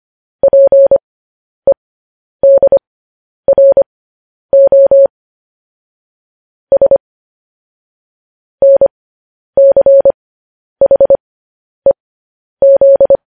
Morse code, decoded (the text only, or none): PEDRO S NCHEZ